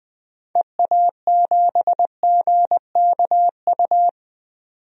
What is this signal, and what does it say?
Morse code, heard as a keyed tone, about 20 wpm, 700 Hz. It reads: EA7GKU
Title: DXCC-100 Call Signs - 6 Characters - 20wpm